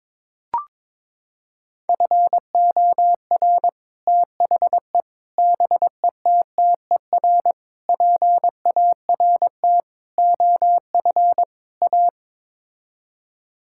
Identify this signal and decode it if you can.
Morse code, heard as a keyed tone, about 22 wpm, 700 Hz. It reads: FOR THE BETTER PART OF A
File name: practice / 22wpm / Sets of 6 Words - 22wpm